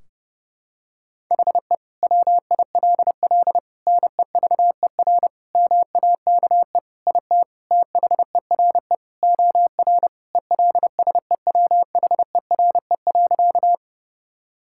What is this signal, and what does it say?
Morse code, heard as a keyed tone, about 30 wpm, 700 Hz. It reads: HE WILL NEVER MAKE IT THERE OR ELSEWHERE.
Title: Sentences from Top 1600 Words 30wpm